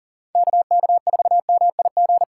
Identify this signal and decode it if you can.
Morse code, heard as a keyed tone, about 40 wpm, 700 Hz. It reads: KK4MIG
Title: USA Callsign DE Callsign 40wpm